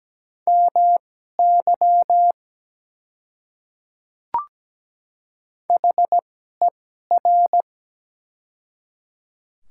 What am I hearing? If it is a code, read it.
Morse code, heard as a keyed tone, about 17 wpm, spaced as for 12 wpm, 700 Hz. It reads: MY HER
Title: 2x Character Spacing - Top 100 Words - 17wpm